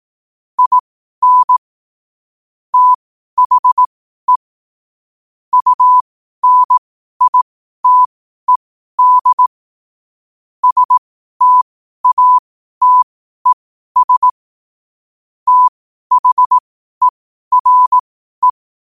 Morse code, heard as a keyed tone, about 18 wpm, spaced as for 12 wpm, 1000 Hz. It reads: IN THE UNITED STATES THERE
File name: qotdfiles / 12wpm